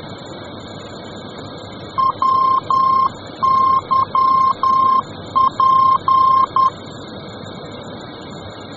Morse code, heard as a keyed tone, about 10 wpm, 1060 Hz. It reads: WYP